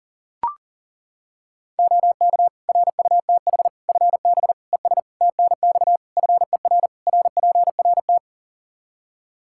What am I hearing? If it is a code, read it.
Morse code, heard as a keyed tone, about 40 wpm, 700 Hz. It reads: OK RUTH FB ES TNX FER RPRT
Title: Practice QSO - Exchange 2 - Info - 40wpm